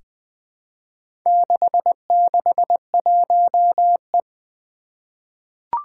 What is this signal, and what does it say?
Morse code, heard as a keyed tone, about 20 wpm, 700 Hz. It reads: 661E
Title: Four Challenging Letter-Number Combinations 20wpm